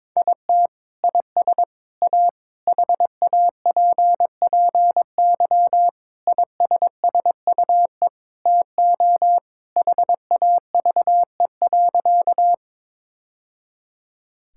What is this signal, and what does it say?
Morse code, heard as a keyed tone, about 22 wpm, 700 Hz. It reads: IT IS A HAPPY ISSUE TO HAVE.
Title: Sentences from Top 1000 Words - Review Part 6 - 22wpm